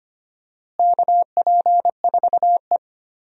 Morse code, heard as a keyed tone, about 25 wpm, 700 Hz. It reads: KP4E
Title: DXCC-100 Call Signs - 4 Characters - Repeated 3x - 25wpm